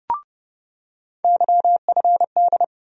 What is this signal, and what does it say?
Morse code, heard as a keyed tone, about 30 wpm, 700 Hz. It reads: YFD
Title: USA Call Sign Postfixes - 3 Letters 30wpm